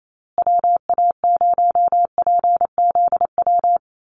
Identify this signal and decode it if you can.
Morse code, heard as a keyed tone, about 28 wpm, 700 Hz. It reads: WA0PZW
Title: USA Call Signs Repeated 3x 28wpm